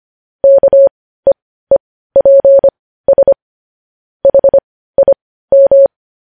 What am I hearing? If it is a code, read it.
Morse code, heard as a keyed tone, about 25 wpm, spaced as for 15 wpm, 550 Hz. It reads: KEEPS HIM
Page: News Headlines in Morse Code at 15 WPM